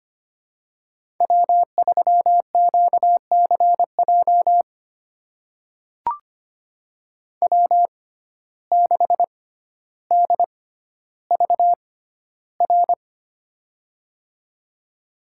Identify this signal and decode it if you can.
Morse code, heard as a keyed tone, about 25 wpm, spaced as for 13 wpm, 700 Hz. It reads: W3QCJ W6DVR
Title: USA Call Signs - 5 Characters - 6x Character Spacing - 25wpm